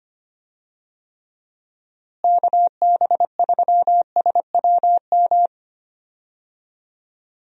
Morse code, heard as a keyed tone, about 25 wpm, 700 Hz. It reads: KB3SWM
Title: USA Call Signs - 6 Characters - Repeated 3x - 25wpm